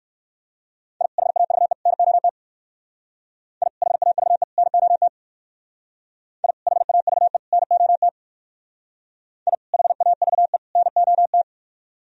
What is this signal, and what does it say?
Morse code, heard as a keyed tone, about 57 wpm, 700 Hz. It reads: I HAVE NOT I HAVE NOT I HAVE NOT I HAVE NOT